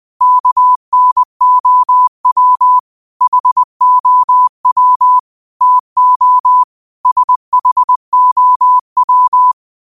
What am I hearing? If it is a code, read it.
Morse code, heard as a keyed tone, about 20 wpm, 1000 Hz. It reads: KNOW HOW TO SHOW
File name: qotdfiles / 20wpm